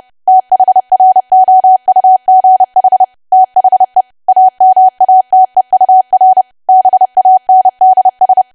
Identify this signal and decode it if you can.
Morse code, heard as a keyed tone, about 30 wpm, 750 Hz. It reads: THROUGH THE AMATEUR BANDS
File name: cwa / W1AW-Level3 / 30WPM